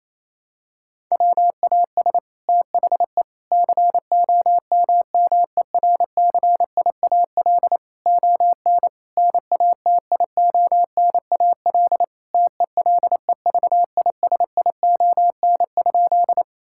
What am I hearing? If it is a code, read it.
Morse code, heard as a keyed tone, about 27 wpm, 700 Hz. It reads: WAS THE COMMERCIAL ON NATIONAL TELEVISION?
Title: Sentences from Top 1000 Words - Review Part 5 - 28wpm